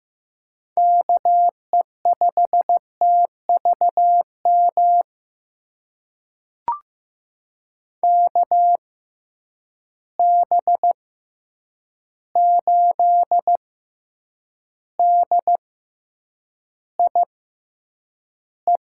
Morse code, heard as a keyed tone, about 15 wpm, spaced as for 8 wpm, 700 Hz. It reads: KE5TVM KB8DIE
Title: USA Call Signs - 6 Characters - 6x Character Spacing - 15wpm